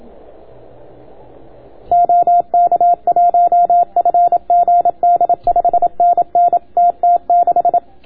Morse code, heard as a keyed tone, about 27 wpm, 690 Hz. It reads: OK1FGD 5NN TT6